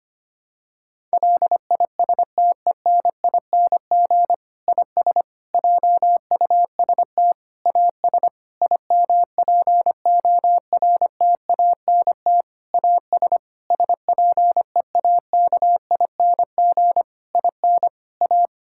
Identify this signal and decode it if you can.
Morse code, heard as a keyed tone, about 25 wpm, 700 Hz. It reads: LISTENING IS JUST AS IMPORTANT AS SPEAKING IN A
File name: practice / 25wpm / Difficult Sentences with Hints - 25wpm